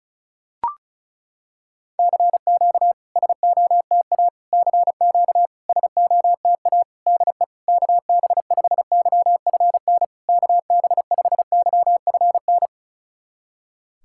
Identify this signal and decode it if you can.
Morse code, heard as a keyed tone, about 36 wpm, 690 Hz. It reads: CQ SOTA CQ SOTA DE KB5YFN KB5YFN
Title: SOTA QSO Practice 35wpm